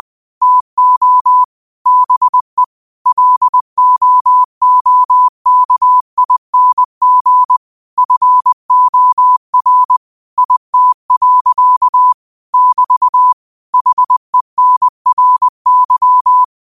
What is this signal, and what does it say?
Morse code, heard as a keyed tone, about 20 wpm, 1000 Hz. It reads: TO BE LOOKING FOR IT. <BT> HENRY